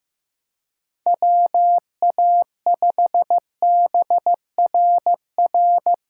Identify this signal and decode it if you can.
Morse code, heard as a keyed tone, about 15 wpm, 700 Hz. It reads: WA5BRR
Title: USA Call Signs - 6 Characters - 6x Character Spacing - 15wpm